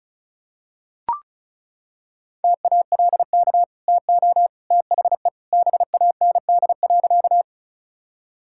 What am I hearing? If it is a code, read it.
Morse code, heard as a keyed tone, about 34 wpm, 700 Hz. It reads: TALK TO THE BAND.
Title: Sentences from Top 1600 Words 35wpm